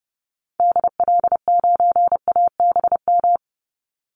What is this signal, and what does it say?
Morse code, heard as a keyed tone, about 30 wpm, 700 Hz. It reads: DL9ABM
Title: Call Signs - 30wpm